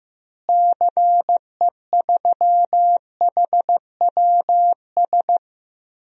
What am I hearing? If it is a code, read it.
Morse code, heard as a keyed tone, about 15 wpm, 700 Hz. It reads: CE3HWS
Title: DXCC-100 Call Signs - 6 Characters - Repeated 3x - 15wpm